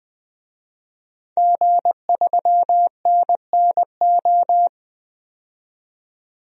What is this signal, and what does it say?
Morse code, heard as a keyed tone, about 20 wpm, 700 Hz. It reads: G3NNO
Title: DXCC-100 Call Signs - 5 Characters - 20wpm